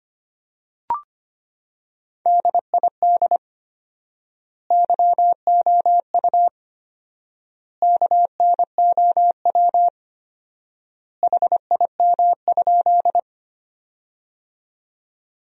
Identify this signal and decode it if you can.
Morse code, heard as a keyed tone, about 25 wpm, 700 Hz. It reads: DID YOU KNOW HIM?
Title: 4x Word-Spacing - Sentences from Top 100 Words - 25wpm